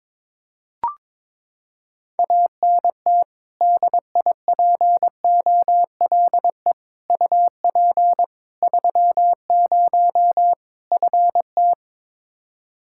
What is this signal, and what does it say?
Morse code, heard as a keyed tone, about 22 wpm, 700 Hz. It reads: ANT DIPOLE UP 30 FT
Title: K1USN SST QSO Practice 22wpm